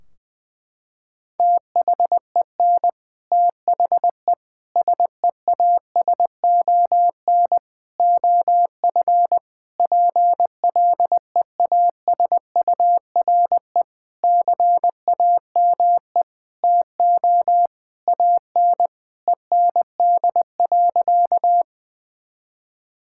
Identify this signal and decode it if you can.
Morse code, heard as a keyed tone, about 20 wpm, 700 Hz. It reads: THEN THE SEASON OF PLEASURE CAME TO AN END.
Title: Sentences from Top 1600 Words 20wpm